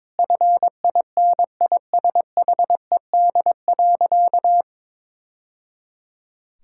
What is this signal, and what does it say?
Morse code, heard as a keyed tone, about 22 wpm, 700 Hz. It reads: FINISHED.
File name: practice / 22wpm / Sentences with 2-Word Phrases - 22wpm